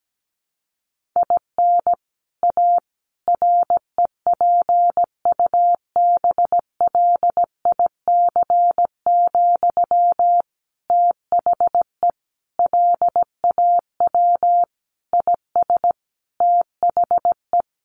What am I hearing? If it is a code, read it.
Morse code, heard as a keyed tone, about 17 wpm, 700 Hz. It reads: IN A REPUBLIC, THE LAW IS THE
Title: Sentences from Top 1500 Words 17wpm